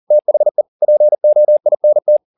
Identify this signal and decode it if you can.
Morse code, heard as a keyed tone, about 40 wpm, 600 Hz. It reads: THE POINT
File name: the old man and the sea 2 40wpm